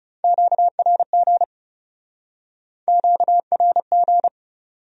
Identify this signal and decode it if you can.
Morse code, heard as a keyed tone, about 40 wpm, 700 Hz. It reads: QRG QRG